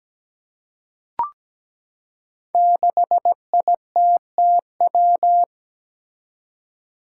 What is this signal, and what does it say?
Morse code, heard as a keyed tone, about 17 wpm, 700 Hz. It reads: 6ITTW
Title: Five Challenging Letter-Number Combinations 17wpm